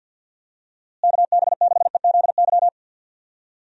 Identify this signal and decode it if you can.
Morse code, heard as a keyed tone, about 45 wpm, 700 Hz. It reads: KB6EZY
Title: USA Call Signs - 6 Characters - Repeated 3x - 50wpm